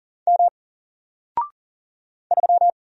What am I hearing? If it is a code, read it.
Morse code, heard as a keyed tone, about 40 wpm, 700 Hz. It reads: M 3
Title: Single Letter-Number 40wpm